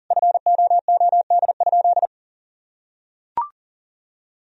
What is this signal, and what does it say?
Morse code, heard as a keyed tone, about 40 wpm, 700 Hz. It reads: FOOD?
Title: Sentences from Top 1000 Words - Review Part 3 - 40wpm